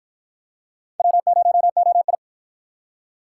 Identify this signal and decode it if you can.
Morse code, heard as a keyed tone, about 52 wpm, 700 Hz. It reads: W0OI